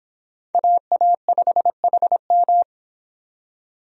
Morse code, heard as a keyed tone, about 25 wpm, 700 Hz. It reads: AA5HM